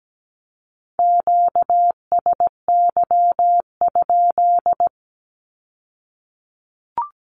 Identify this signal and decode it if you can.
Morse code, heard as a keyed tone, about 17 wpm, 700 Hz. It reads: QSY?
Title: Q-Codes with Spoken Meaning - 17wpm